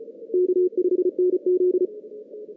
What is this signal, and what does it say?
Morse code, heard as a keyed tone, about 34 wpm, 360 Hz. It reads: K5NZ
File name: 2016 JAN CW NAQP / 2016 Jan naqp k5ot - 0-8